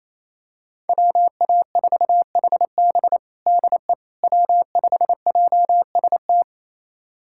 Morse code, heard as a keyed tone, about 27 wpm, 700 Hz. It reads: WA4HB DE W5JST
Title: USA Callsign DE Callsign - 5 Characters - 28wpm